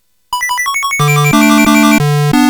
There is no code.